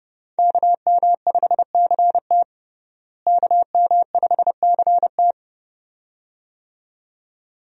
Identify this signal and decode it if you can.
Morse code, heard as a keyed tone, about 30 wpm, 700 Hz. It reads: KM5CT KM5CT
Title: USA Call Signs - 5 Characters - Repeated 2x - 30wpm